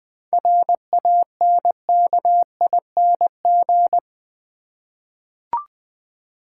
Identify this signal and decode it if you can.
Morse code, heard as a keyed tone, about 20 wpm, 700 Hz. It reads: RANKING
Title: 7 Letter Words - First 3 Letters Given - 20wpm